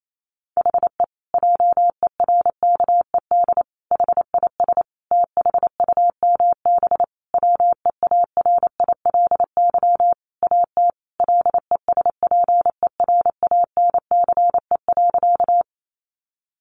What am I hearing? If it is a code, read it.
Morse code, heard as a keyed tone, about 28 wpm, 700 Hz. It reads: HE JERKED HIS THUMB WEARILY AT LESPERANCE.
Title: A Sound Of Thunder 28wpm